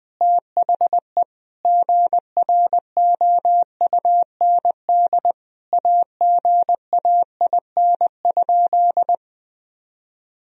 Morse code, heard as a keyed tone, about 20 wpm, 700 Hz. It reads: THE GROUND AGAIN?
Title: Sentences from Top 1000 Words - Review Part 2 - 20wpm